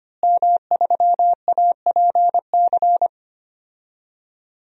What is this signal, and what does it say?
Morse code, heard as a keyed tone, about 25 wpm, 700 Hz. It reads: M3APC
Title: DXCC-100 Call Signs - 5 Characters - Repeated 3x - 25wpm